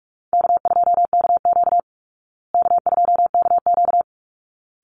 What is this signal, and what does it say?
Morse code, heard as a keyed tone, about 44 wpm, 700 Hz. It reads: K2KQ K2KQ
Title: Easy Call Signs Repeated 2x - 45wpm